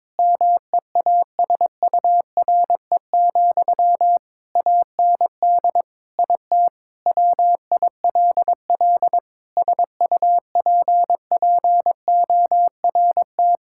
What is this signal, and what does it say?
Morse code, heard as a keyed tone, about 22 wpm, 700 Hz. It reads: MEASURE, AND IT WILL SUPPORT
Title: Sentences from Top 1000 Words - Review Part 5 - 22wpm